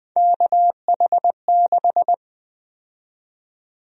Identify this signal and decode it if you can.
Morse code, heard as a keyed tone, about 20 wpm, 700 Hz. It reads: KH6
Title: USA Call Sign Prefixes - 20wpm